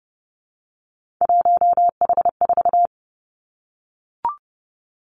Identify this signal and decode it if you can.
Morse code, heard as a keyed tone, about 31 wpm, 700 Hz. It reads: 1H4
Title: Three Challenging Letter-Number Combinations 30wpm